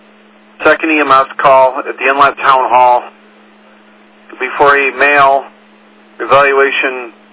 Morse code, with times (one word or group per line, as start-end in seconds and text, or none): none